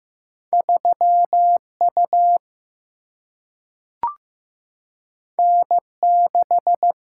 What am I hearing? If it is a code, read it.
Morse code, heard as a keyed tone, about 15 wpm, 700 Hz. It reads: 3 U N 6
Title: Single-Letter Single-Number 15wpm